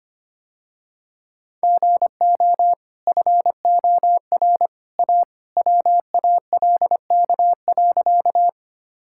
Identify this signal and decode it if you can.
Morse code, heard as a keyed tone, about 25 wpm, 700 Hz. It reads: GO FOR A WALK.
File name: Easy Sentences with Hints - 25wpm